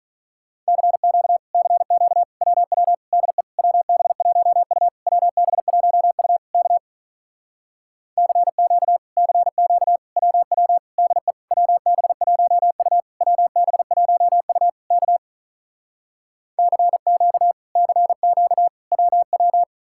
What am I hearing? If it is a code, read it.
Morse code, heard as a keyed tone, about 46 wpm, 700 Hz. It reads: CQ CQ WW DE WB1U WB1U K CQ CQ WW DE WB1U WB1U K CQ CQ WW